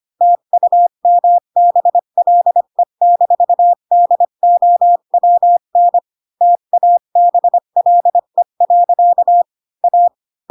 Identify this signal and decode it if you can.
Morse code, heard as a keyed tone, about 25 wpm, 700 Hz. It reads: TUMBLE-DOWN TABLE. A